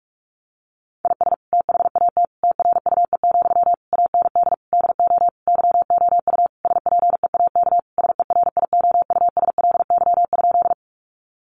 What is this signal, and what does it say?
Morse code, heard as a keyed tone, about 44 wpm, 700 Hz. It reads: IS THAT TRUE, AND DO YOU SPEAK SERIOUSLY?
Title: Sentences from Top 1200 Words 45wpm